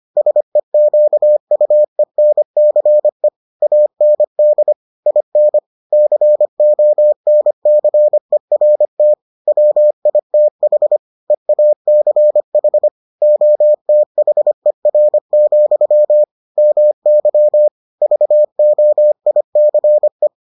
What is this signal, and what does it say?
Morse code, heard as a keyed tone, about 25 wpm, 600 Hz. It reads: SEQUENCE AND IN CONCERT WITH EACH OTHER, MY VOICE